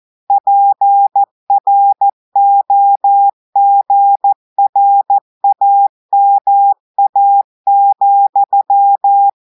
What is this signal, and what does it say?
Morse code, heard as a keyed tone, about 14 wpm, 800 Hz. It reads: PROGRAMA,
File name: audios / portugais / portugais texte en clair 2 vitesse 14 mots minutes